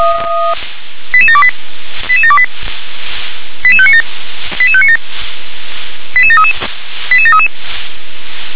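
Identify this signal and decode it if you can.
Five-tone selective call (ZVEI), heard as quick runs of five tones, tones of about 70 ms. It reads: 88728 88728 88597 88597 88733 88733
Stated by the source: ZVEI five-tone call received on SDR.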